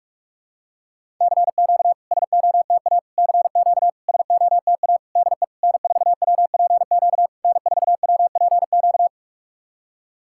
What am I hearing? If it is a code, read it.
Morse code, heard as a keyed tone, about 44 wpm, 660 Hz. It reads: CQ SOTA CQ SOTA DE N4WPQ N4WPQ
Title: SOTA QSO Practice 45wpm